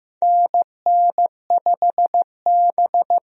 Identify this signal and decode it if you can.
Morse code, heard as a keyed tone, about 15 wpm, 700 Hz. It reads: NN5B
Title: USA Call Signs - 4 Characters - Repeated 2x - 15wpm